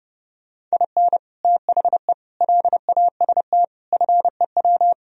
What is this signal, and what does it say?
Morse code, heard as a keyed tone, about 33 wpm, 700 Hz. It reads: IN THE LAST FEW